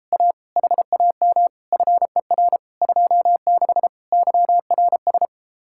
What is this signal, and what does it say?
Morse code, heard as a keyed tone, about 33 wpm, 700 Hz. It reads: A HAM FER 26 YRS